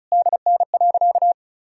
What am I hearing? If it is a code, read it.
Morse code, heard as a keyed tone, about 36 wpm, 700 Hz. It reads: DN.
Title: Sentences from Top 500 Words and Abbreviations 35wpm